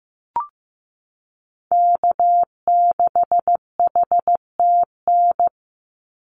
Morse code, heard as a keyed tone, about 15 wpm, 700 Hz. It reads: K6HTN
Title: Call Signs Repeated 3x - 15wpm